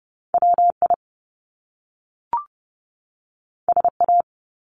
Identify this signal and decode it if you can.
Morse code, heard as a keyed tone, about 30 wpm, 700 Hz. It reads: WI SA
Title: Top Two Letter Combinations in Words 30wpm